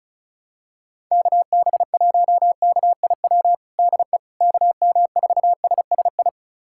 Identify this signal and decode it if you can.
Morse code, heard as a keyed tone, about 36 wpm, 700 Hz. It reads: KB1KIW DE KM4SSI
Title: USA Callsign DE Callsign 35wpm